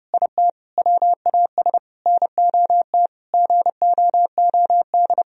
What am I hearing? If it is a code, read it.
Morse code, heard as a keyed tone, about 30 wpm, 700 Hz. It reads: IT WAS NOT GOOD
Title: Sentences from Top 1000 Words - Review Part 4 - 30wpm